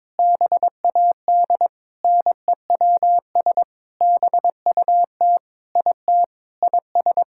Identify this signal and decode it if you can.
Morse code, heard as a keyed tone, about 22 wpm, 700 Hz. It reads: BAD NEWS BUT IT IS
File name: Sentences with 9-Word Phrases - 22wpm